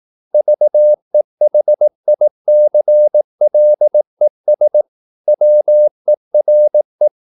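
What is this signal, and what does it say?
Morse code, heard as a keyed tone, about 18 wpm, 600 Hz. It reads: VEHICLES WERE
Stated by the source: Morse practice at 18 wpm, English text, from CW med Gnister.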